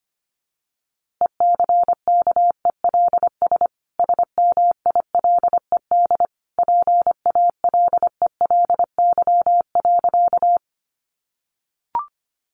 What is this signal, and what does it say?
Morse code, heard as a keyed tone, about 25 wpm, 700 Hz. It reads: ECKELS SMILED PALELY.
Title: A Sound Of Thunder 25wpm